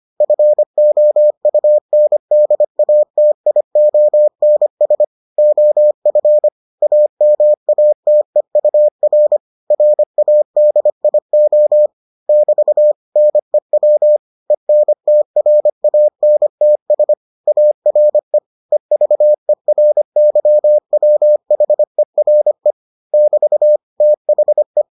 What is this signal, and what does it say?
Morse code, heard as a keyed tone, about 25 wpm, 600 Hz. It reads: FOUNDATIONS OF AMATEUR RADIO <BT> NEW ENTRANTS ARE EVERYWHERE <BT> THE